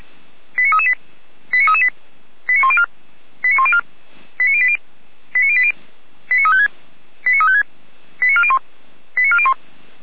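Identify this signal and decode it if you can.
Five-tone selective call (ZVEI), heard as quick runs of five tones, tones of about 80 ms. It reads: 89338 89338 89104 89104 89080 89080 89356 89356 89401 89401
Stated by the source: Five-tone ZVEI selective call 89126, received on SDR.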